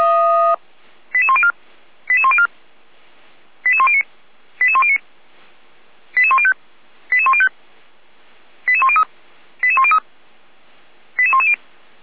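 Five-tone selective call (ZVEI), heard as quick runs of five tones, tones of about 70 ms. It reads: 88104 88104 88109 88109 88105 88105 88103 88103 88110
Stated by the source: ZVEI five-tone selective call 88405 received by SDR.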